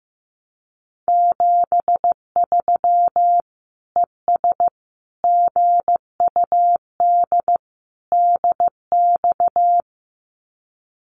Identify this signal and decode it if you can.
Morse code, heard as a keyed tone, about 15 wpm, 700 Hz. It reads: 73 ES GUD DX
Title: Common QSO Phrases 15wpm